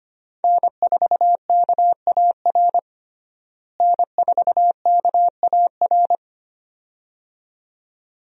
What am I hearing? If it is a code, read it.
Morse code, heard as a keyed tone, about 25 wpm, 700 Hz. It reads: N4KAR N4KAR
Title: USA Call Signs - 5 Characters - Repeated 2x - 25wpm